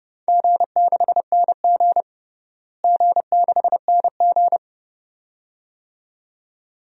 Morse code, heard as a keyed tone, about 30 wpm, 700 Hz. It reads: G6NG G6NG
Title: DXCC-100 Call Signs - 4 Characters - Repeated 2x - 30wpm